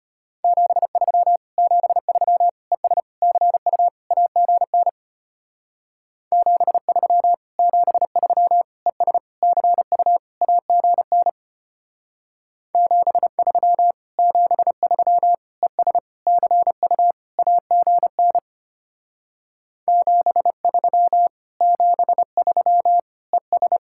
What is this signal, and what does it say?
Morse code, heard as a keyed tone, about 32 wpm, 700 Hz. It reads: 73 73 ES CU AGN 73 73 ES CU AGN 73 73 ES CU AGN 73 73 ES